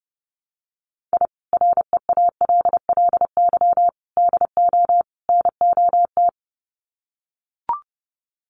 Morse code, heard as a keyed tone, about 30 wpm, 700 Hz. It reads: I REALLY DO NOT
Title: Sets of 4 Words 30wpm